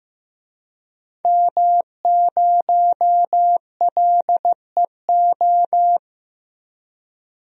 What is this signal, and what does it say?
Morse code, heard as a keyed tone, about 15 wpm, 700 Hz. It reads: M0LEO
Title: DXCC-100 Call Signs - 5 Characters - 15wpm